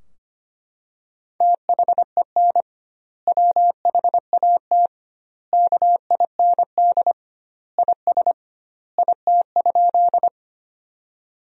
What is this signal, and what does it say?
Morse code, heard as a keyed tone, about 25 wpm, 700 Hz. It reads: THEN WHAT KIND IS IT?
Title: 2x Word-Spacing - Sentences from Top 200 Words - 25wpm